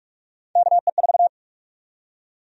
Morse code, heard as a keyed tone, about 46 wpm, 700 Hz. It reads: KE4
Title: USA Call Sign Prefixes - 45wpm